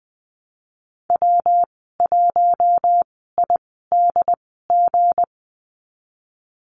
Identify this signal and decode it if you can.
Morse code, heard as a keyed tone, about 20 wpm, spaced as for 15 wpm, 700 Hz. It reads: W1IDG
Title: 167 Lesson 27g - Number 7 - Call Signs 20wpm